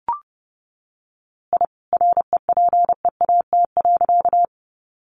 Morse code, heard as a keyed tone, about 30 wpm, 700 Hz. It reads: I REPEAT.
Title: A Sound Of Thunder 30wpm